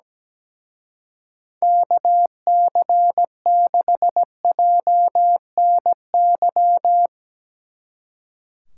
Morse code, heard as a keyed tone, about 17 wpm, 700 Hz. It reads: KC6JNY
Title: Speed Racing - USA 2x3 Call Signs 17wpm